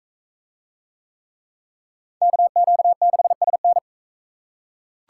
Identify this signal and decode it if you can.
Morse code, heard as a keyed tone, about 39 wpm, 700 Hz. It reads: KQ6SN